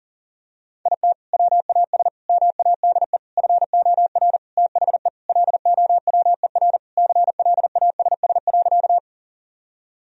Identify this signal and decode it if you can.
Morse code, heard as a keyed tone, about 40 wpm, 700 Hz. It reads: IT WAS MADE FOR THE LOWER CLASS.
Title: Sentences from Top 1000 Words - Review Part 4 - 40wpm